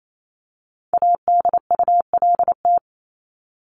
Morse code, heard as a keyed tone, about 28 wpm, 700 Hz. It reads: ADULT
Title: Top 1501-1750 Words 28wpm